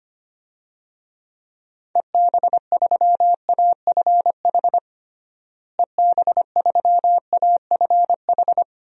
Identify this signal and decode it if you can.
Morse code, heard as a keyed tone, about 25 wpm, 700 Hz. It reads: EB3AFH EB3AFH
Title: DXCC-100 Call Signs - 6 Characters - Repeated 2x - 25wpm